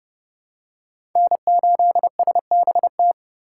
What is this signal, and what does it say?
Morse code, heard as a keyed tone, about 29 wpm, 700 Hz. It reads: N8SBT